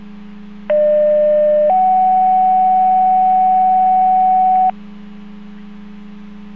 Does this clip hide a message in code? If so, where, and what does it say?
Yes: two-tone page from 0.7 to 4.7 s, heard as a one-second tone, then a three-second tone, about 600 then 750 Hz.